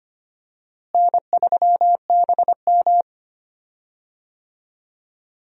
Morse code, heard as a keyed tone, about 25 wpm, 700 Hz. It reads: N3BM
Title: USA Call Signs - 4 Characters - Repeated 2x - 25wpm